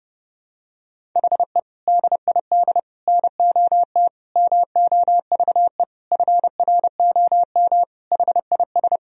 Morse code, heard as a keyed tone, about 30 wpm, 700 Hz. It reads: HE DID NOT MOVE FROM HIS
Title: Sentences from Top 1000 Words - Review Part 1 - 30wpm